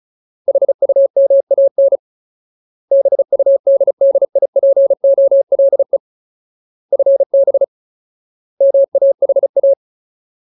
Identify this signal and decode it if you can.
Morse code, heard as a keyed tone, about 34 wpm, 550 Hz. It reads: HUMAN BUDDIPOLE FB MAHA
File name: Ultimate Word List / 35WPM Ultimate